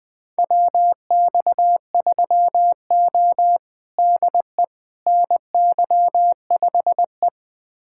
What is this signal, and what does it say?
Morse code, heard as a keyed tone, about 20 wpm, 700 Hz. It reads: WX3O DE NY5E
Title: USA Callsign DE Callsign - 4 Characters - 20wpm